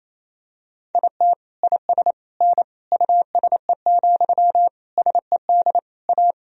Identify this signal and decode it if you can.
Morse code, heard as a keyed tone, about 28 wpm, 700 Hz. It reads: IT IS N USE, SED A
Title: Sentences from Top 500 Words and Abbreviations 28wpm